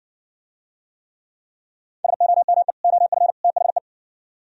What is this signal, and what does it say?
Morse code, heard as a keyed tone, about 60 wpm, 700 Hz. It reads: SOME OF THE